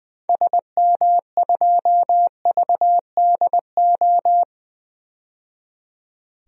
Morse code, heard as a keyed tone, about 20 wpm, 700 Hz. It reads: SM2VDO
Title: DXCC-100 Call Signs - 6 Characters - 20wpm